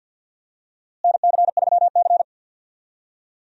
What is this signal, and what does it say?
Morse code, heard as a keyed tone, about 52 wpm, 700 Hz. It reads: NC3C
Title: USA Call Signs - 4 Characters - Repeated 3x - 50wpm